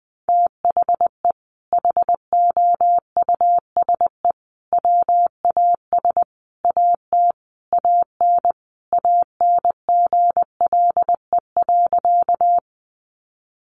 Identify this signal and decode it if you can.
Morse code, heard as a keyed tone, about 20 wpm, 700 Hz. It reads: THE HOUSE WAS AT AN ANGLE.
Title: Sentences from Top 1500 Words 20wpm